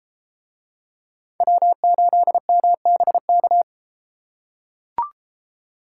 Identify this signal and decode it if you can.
Morse code, heard as a keyed tone, about 33 wpm, 700 Hz. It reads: W8MBK